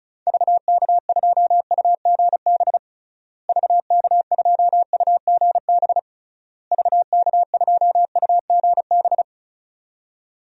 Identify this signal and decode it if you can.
Morse code, heard as a keyed tone, about 36 wpm, 700 Hz. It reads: VK2UGB VK2UGB VK2UGB
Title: DXCC-100 Call Signs - 6 Characters - Repeated 3x - 35wpm